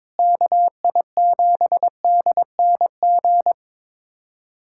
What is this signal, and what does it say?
Morse code, heard as a keyed tone, about 22 wpm, 700 Hz. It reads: KI7DNG
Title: USA Callsign DE Callsign - 6 Characters - 22wpm